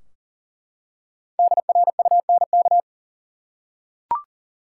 Morse code, heard as a keyed tone, about 40 wpm, 700 Hz. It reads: DRUNK
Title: Five Letter Words - 40wpm